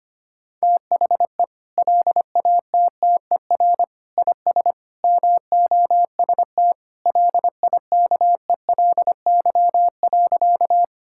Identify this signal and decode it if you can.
Morse code, heard as a keyed tone, about 25 wpm, 700 Hz. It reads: THE LATTER IS MOST LIKELY.
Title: Sentences from Top 1600 Words 25wpm